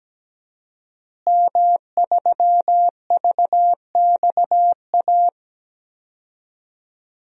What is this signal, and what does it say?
Morse code, heard as a keyed tone, about 17 wpm, 700 Hz. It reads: M3VXA
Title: DXCC-100 Call Signs - 5 Characters - Repeated 2x - 17wpm